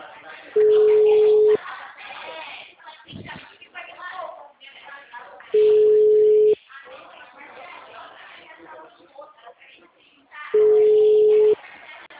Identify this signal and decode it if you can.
Ringback tone (Europe, 425 Hz), with speech.